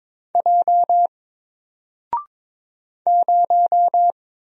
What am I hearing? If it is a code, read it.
Morse code, heard as a keyed tone, about 22 wpm, 700 Hz. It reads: J 0
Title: Single Letter-Number Alt 22wpm